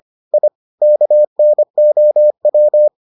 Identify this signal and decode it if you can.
Morse code, heard as a keyed tone, about 25 wpm, 600 Hz. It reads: I KNOW